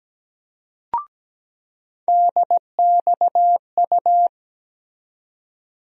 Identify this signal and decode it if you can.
Morse code, heard as a keyed tone, about 17 wpm, 700 Hz. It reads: DXU
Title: USA Call Sign Postfixes - 3 Letters 17wpm